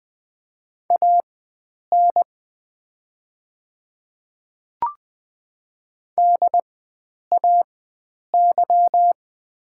Morse code, heard as a keyed tone, about 20 wpm, spaced as for 9 wpm, 690 Hz. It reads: AN DAY
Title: 4x Character Spacing - Top 100 Words - 20wpm